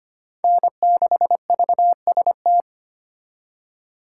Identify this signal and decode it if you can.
Morse code, heard as a keyed tone, about 32 wpm, 700 Hz. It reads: N6VST